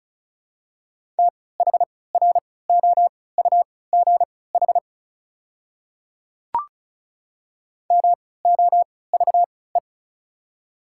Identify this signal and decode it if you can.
Morse code, heard as a keyed tone, about 36 wpm, spaced as for 20 wpm, 700 Hz. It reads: THROUGH MOVE
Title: 3x Character Spacing - Top 300 Words - 35wpm